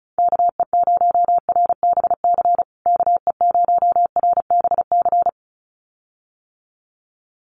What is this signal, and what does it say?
Morse code, heard as a keyed tone, about 34 wpm, 700 Hz. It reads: KE0RBC KE0RBC
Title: USA Call Signs Repeated 2x 35wpm